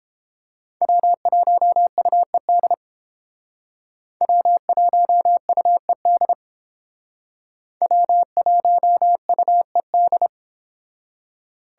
Morse code, heard as a keyed tone, about 29 wpm, 700 Hz. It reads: W1UED W1UED W1UED